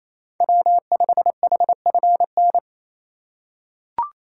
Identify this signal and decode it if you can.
Morse code, heard as a keyed tone, about 28 wpm, 690 Hz. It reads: W5HFN
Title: USA Call Signs - 5 Characters - 4x Character Spacing - 28wpm